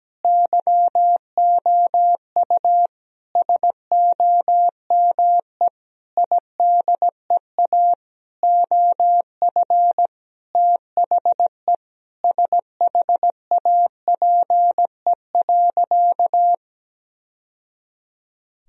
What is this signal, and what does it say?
Morse code, heard as a keyed tone, about 17 wpm, 700 Hz. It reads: YOU SOME IDEA OF THE SHAPE.
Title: Sentences from Top 1000 Words - Review Part 3 - 17wpm